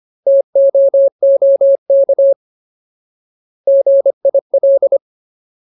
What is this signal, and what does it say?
Morse code, heard as a keyed tone, about 25 wpm, 550 Hz. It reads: TOOK GIL
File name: Ultimate Word List / 25WPM Ultimate